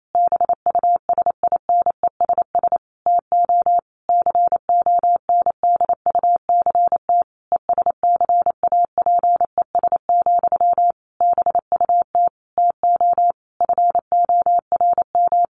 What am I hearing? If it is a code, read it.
Morse code, heard as a keyed tone, about 28 wpm, 700 Hz. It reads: BUSINESS TO CONDUCT ESCAPES, BUT TO FORM
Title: A Sound Of Thunder 28wpm